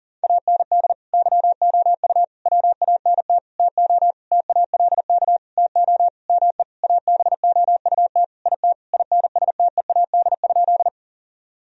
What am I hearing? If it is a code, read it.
Morse code, heard as a keyed tone, about 40 wpm, 700 Hz. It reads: AND YOU WANT TO TALK TO ME ABOUT IT INSTEAD?
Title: Sentences from Top 1000 Words - Review Part 4 - 40wpm